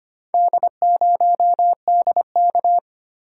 Morse code, heard as a keyed tone, about 25 wpm, 700 Hz. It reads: D0DK
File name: practice / 25wpm / Four Letter-Number Combinations - 25wpm